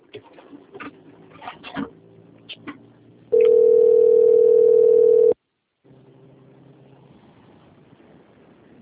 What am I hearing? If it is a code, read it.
Ringback tone (North America), 440 + 480 Hz.
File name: out- - - - - .3100